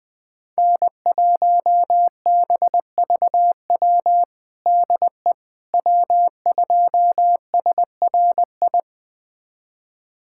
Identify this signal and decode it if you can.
Morse code, heard as a keyed tone, about 20 wpm, 700 Hz. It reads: N1BVW DE W2SRI
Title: USA Callsign DE Callsign - 5 Characters - 20wpm